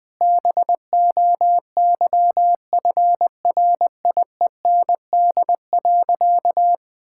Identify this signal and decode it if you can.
Morse code, heard as a keyed tone, about 20 wpm, 700 Hz. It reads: BOYFRIEND.
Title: Sentences from Top 1000 Words - Review Part 5 - 20wpm